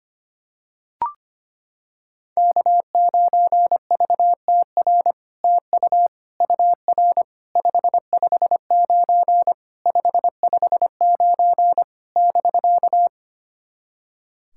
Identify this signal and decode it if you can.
Morse code, heard as a keyed tone, about 25 wpm, 700 Hz. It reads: K9VTR TU UR 559 559 <BK>
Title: SOTA QSO Practice 25wpm